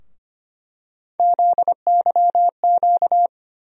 Morse code, heard as a keyed tone, about 25 wpm, 700 Hz. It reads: ZYQ
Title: USA Call Sign Postfixes - 3 Letters 25wpm